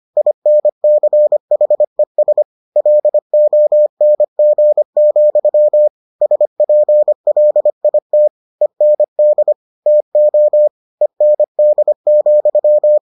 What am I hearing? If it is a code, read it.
Morse code, heard as a keyed tone, about 25 wpm, 600 Hz. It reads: INCHES LONG, SPLIT END TO END,